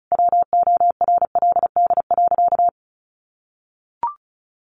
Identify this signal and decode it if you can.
Morse code, heard as a keyed tone, about 34 wpm, 700 Hz. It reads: WORLD.
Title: Sentences from Top 800 Words 35wpm